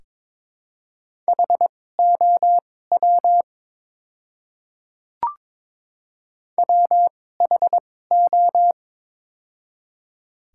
Morse code, heard as a keyed tone, about 22 wpm, spaced as for 16 wpm, 700 Hz. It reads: HOW WHO
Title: 2x Character Spacing - Top 100 Words - 22wpm